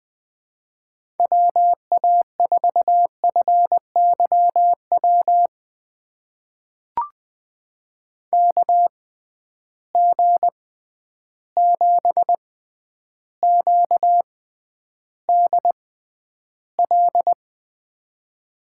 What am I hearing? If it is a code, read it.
Morse code, heard as a keyed tone, about 20 wpm, spaced as for 10 wpm, 700 Hz. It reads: WA4FYW KG7QDL
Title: USA Call Signs - 6 Characters - 6x Character Spacing - 20wpm